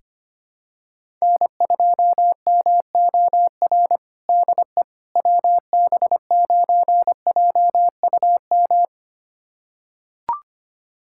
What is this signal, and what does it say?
Morse code, heard as a keyed tone, about 25 wpm, 700 Hz. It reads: N2MOR DE WB9JUM
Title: USA Callsign DE Callsign 25wpm